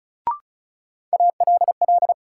Morse code, heard as a keyed tone, about 34 wpm, 700 Hz. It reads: ALL
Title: Sentences from Top 100 Words - New Word Drill - 35wpm